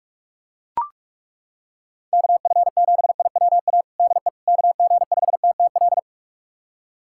Morse code, heard as a keyed tone, about 41 wpm, 700 Hz. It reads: KF7IWA DE KG5TTL
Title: USA Callsign DE Callsign 45wpm